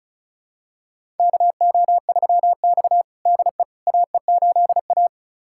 Morse code, heard as a keyed tone, about 36 wpm, 700 Hz. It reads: KO3X DE AE8A
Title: USA Callsign DE Callsign - 4 Characters - 35wpm